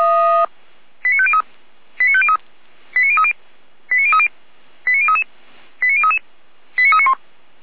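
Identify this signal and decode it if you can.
Five-tone selective call (ZVEI), heard as quick runs of five tones, tones of about 70 ms. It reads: 89503 89503 89030 89030 89033 89033 89391
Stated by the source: ZVEI five-tone selective call received with an SDR for Feuerwehr Orlamünde, a fire brigade.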